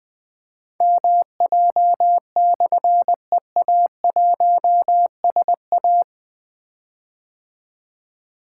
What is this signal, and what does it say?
Morse code, heard as a keyed tone, about 21 wpm, 700 Hz. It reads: MJ/EA1SA
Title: Difficult Call Signs 20wpm